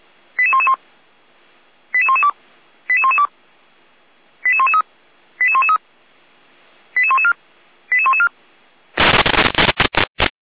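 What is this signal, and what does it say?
Five-tone selective call (ZVEI), heard as quick runs of five tones, tones of about 70 ms. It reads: 88111 88112 88112 88113 88113 88114 88114
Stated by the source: ZVEI five-tone selective call 88497 received on SDR, from the audio server of Feuerwehr Orlamünde (fire brigade).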